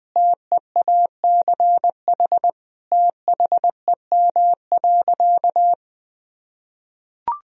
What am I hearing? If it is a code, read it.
Morse code, heard as a keyed tone, about 20 wpm, 700 Hz. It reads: TEACH THEM.
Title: Sentences from Top 1000 Words - Review Part 2 - 20wpm